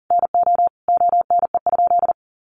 Morse code, heard as a keyed tone, about 40 wpm, 700 Hz. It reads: NO ONE?
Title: Sentences from Top 600 Words 40wpm